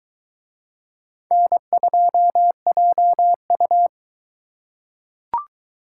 Morse code, heard as a keyed tone, about 22 wpm, 700 Hz. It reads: N2JU